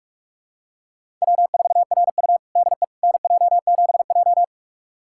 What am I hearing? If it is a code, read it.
Morse code, heard as a keyed tone, about 46 wpm, 700 Hz. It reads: W4RU DE NJ7J
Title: USA Callsign DE Callsign - 4 Characters - 45wpm